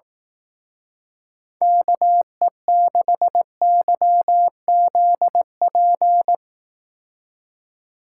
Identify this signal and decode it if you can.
Morse code, heard as a keyed tone, about 18 wpm, 700 Hz. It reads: KE6YZP
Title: Speed Racing - USA 2x3 Call Signs 15wpm